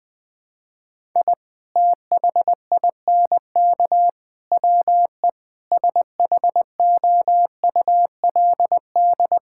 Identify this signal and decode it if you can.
Morse code, heard as a keyed tone, about 20 wpm, 700 Hz. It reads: I THINK WE SHOULD
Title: Sentences with 3-Word Phrases - 20wpm